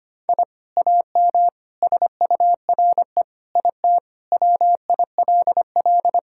Morse code, heard as a keyed tone, about 25 wpm, 700 Hz. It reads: I AM SURE IT WILL
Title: Sentences from Top 1000 Words - Review Part 5 - 25wpm